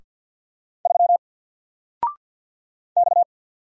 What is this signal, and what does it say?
Morse code, heard as a keyed tone, about 50 wpm, 700 Hz. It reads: 3 X
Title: Single Letter-Number 50wpm